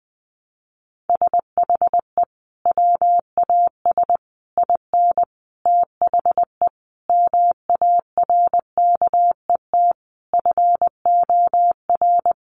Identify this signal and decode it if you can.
Morse code, heard as a keyed tone, about 20 wpm, 700 Hz. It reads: SHE WAS IN THE MARKET FOR
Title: Sentences from Top 600 Words 20wpm